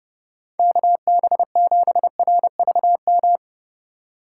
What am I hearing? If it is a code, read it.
Morse code, heard as a keyed tone, about 39 wpm, 700 Hz. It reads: KB7RVM